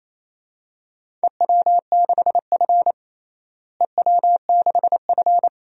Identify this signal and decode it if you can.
Morse code, heard as a keyed tone, about 28 wpm, 700 Hz. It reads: EW6F EW6F
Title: DXCC-100 Call Signs - 4 Characters - Repeated 2x - 28wpm